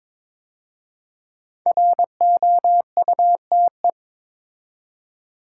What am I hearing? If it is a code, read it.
Morse code, heard as a keyed tone, about 22 wpm, 700 Hz. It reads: ROUTE
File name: 5 Letter Words - First 3 Letters Given - 22wpm